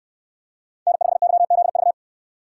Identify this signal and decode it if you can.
Morse code, heard as a keyed tone, about 67 wpm, 700 Hz. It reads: N5YZV